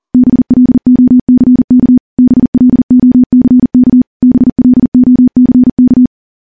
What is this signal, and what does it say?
Morse code, heard as a keyed tone, about 40 wpm, 250 Hz. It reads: BLOCK BLOCK BLOCK
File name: morse / morse-training-40WPM